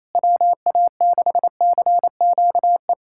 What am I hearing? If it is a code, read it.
Morse code, heard as a keyed tone, about 28 wpm, 700 Hz. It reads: WA6CQE
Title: USA Callsign DE Callsign 28wpm